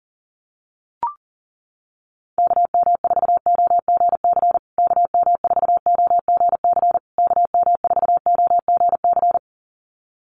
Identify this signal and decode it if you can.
Morse code, heard as a keyed tone, about 40 wpm, 700 Hz. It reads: KM4OGC KM4OGC KM4OGC
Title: USA Call Signs Repeated 3x 40wpm